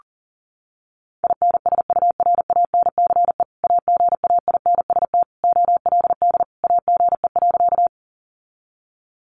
Morse code, heard as a keyed tone, about 37 wpm, 700 Hz. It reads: INSURANCE AGAINST OLD AGE.
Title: Sentences from Top 1100 Words 40wpm